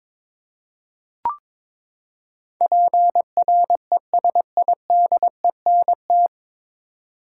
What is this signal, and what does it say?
Morse code, heard as a keyed tone, about 22 wpm, 700 Hz. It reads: PRESIDENT
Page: Morse Code Ninja practice 22wpm